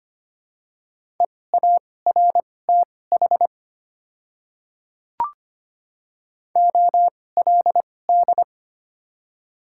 Morse code, heard as a keyed tone, about 25 wpm, spaced as for 18 wpm, 700 Hz. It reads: EARTH OLD
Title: 2x Character Spacing - Top 200 Words - 25wpm